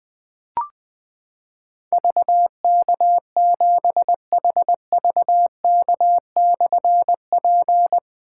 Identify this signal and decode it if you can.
Morse code, heard as a keyed tone, about 21 wpm, 700 Hz. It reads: VK7HVK/P
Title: Difficult Call Signs 20wpm